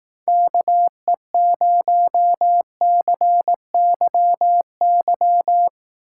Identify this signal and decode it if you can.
Morse code, heard as a keyed tone, about 18 wpm, 700 Hz. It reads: KE0CYY
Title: Speed Racing - USA 2x3 Call Signs 15wpm